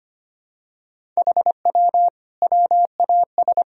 Morse code, heard as a keyed tone, about 25 wpm, 700 Hz. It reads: HW WAS
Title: Sentences from Top 100 Words and Abbreviations 25wpm